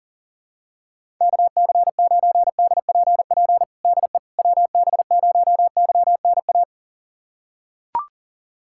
Morse code, heard as a keyed tone, about 40 wpm, 700 Hz. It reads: KC9DPP DE WB0YNA
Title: USA Callsign DE Callsign - 6 Characters - 40wpm